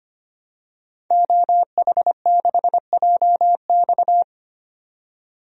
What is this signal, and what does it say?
Morse code, heard as a keyed tone, about 25 wpm, 700 Hz. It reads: OH6JX
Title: DXCC-100 Call Signs - 5 Characters - 25wpm